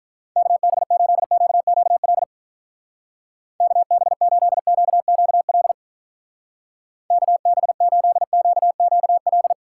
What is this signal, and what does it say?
Morse code, heard as a keyed tone, about 46 wpm, 700 Hz. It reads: KB8QQL KB8QQL KB8QQL